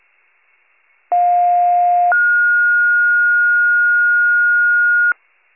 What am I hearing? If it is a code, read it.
Two-tone page, heard as a one-second tone, then a three-second tone, about 710 then 1430 Hz. It is heard from 1.1 to 5.1 s.